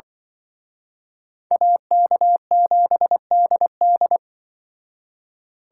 Morse code, heard as a keyed tone, about 24 wpm, 700 Hz. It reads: AK7DD
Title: Speed Racing - USA 2x2 Call Signs 20wpm